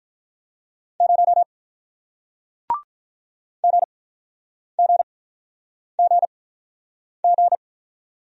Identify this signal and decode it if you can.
Morse code, heard as a keyed tone, about 47 wpm, spaced as for 9 wpm, 700 Hz. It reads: 0 GGGG